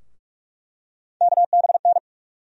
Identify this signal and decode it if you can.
Morse code, heard as a keyed tone, about 42 wpm, 700 Hz. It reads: KBN
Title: USA Call Sign Postfixes - 3 Letters 45wpm